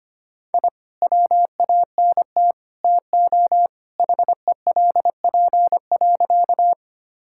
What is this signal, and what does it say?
Morse code, heard as a keyed tone, about 25 wpm, 700 Hz. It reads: I WANT TO HELP.
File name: practice / 25wpm / Sentences with 3-Word Phrases - 25wpm